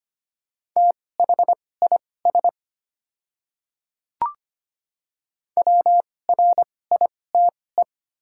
Morse code, heard as a keyed tone, about 25 wpm, spaced as for 18 wpm, 700 Hz. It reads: THIS WRITE
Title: 2x Character Spacing - Top 100 Words - 25wpm